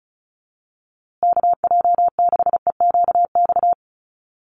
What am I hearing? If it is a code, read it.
Morse code, heard as a keyed tone, about 34 wpm, 700 Hz. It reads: KJ6EQX
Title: USA Call Signs Repeated 2x 35wpm